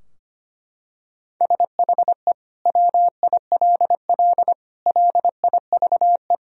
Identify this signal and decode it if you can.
Morse code, heard as a keyed tone, about 25 wpm, 700 Hz. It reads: SHE WILL LIVE
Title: Sentences from Top 1000 Words - Review Part 4 - 25wpm